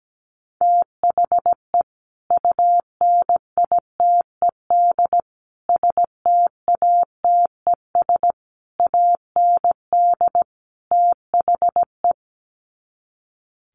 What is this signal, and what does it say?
Morse code, heard as a keyed tone, about 17 wpm, 700 Hz. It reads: THE UNITED STATES AND THE
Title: Sets of 5 Words 17wpm